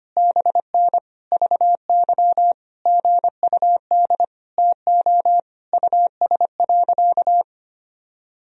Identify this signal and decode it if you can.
Morse code, heard as a keyed tone, about 25 wpm, 700 Hz. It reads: BN VY GUD TO US.
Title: Sentences from Top 200 Words and Abbreviations 25wpm